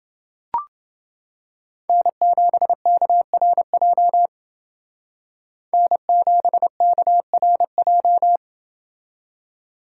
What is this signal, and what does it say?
Morse code, heard as a keyed tone, about 28 wpm, 700 Hz. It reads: N7KRJ N7KRJ